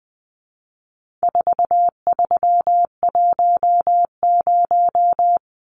Morse code, heard as a keyed tone, about 20 wpm, 700 Hz. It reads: 4310
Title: Numbers - 4-Digit 20wpm